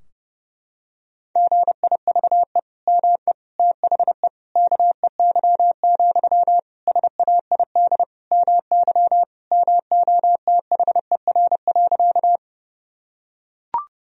Morse code, heard as a keyed tone, about 32 wpm, 700 Hz. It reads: GIVE ME THE KEY, SAID MY MOTHER.
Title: Sentences from Top 1000 Words - Review Part 4 - 30wpm